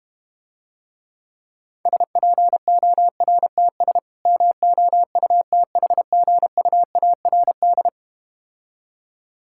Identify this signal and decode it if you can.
Morse code, heard as a keyed tone, about 33 wpm, 700 Hz. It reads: SPORTS MOUTHGUARD